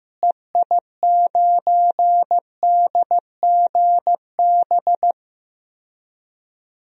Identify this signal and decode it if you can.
Morse code, heard as a keyed tone, about 15 wpm, 700 Hz. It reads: EI9DGB
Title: DXCC-100 Call Signs - 6 Characters - Repeated 2x - 15wpm